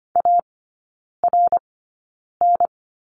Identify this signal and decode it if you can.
Morse code, heard as a keyed tone, about 25 wpm, spaced as for 9 wpm, 700 Hz. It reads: ARN
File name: course / 25:10wpm / 093 Lesson 16e - Letter c - Three Characters 25wpm